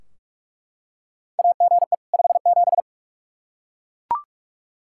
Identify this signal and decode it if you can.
Morse code, heard as a keyed tone, about 41 wpm, 700 Hz. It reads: AGE 57
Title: Practice QSO - Exchange 3 - More Info - 45wpm